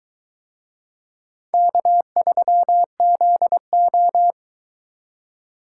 Morse code, heard as a keyed tone, about 20 wpm, 700 Hz. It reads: K3ZO